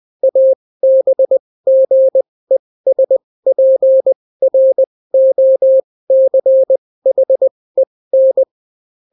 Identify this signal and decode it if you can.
Morse code, heard as a keyed tone, about 20 wpm, spaced as for 16 wpm, 530 Hz. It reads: ABGESPROCHEN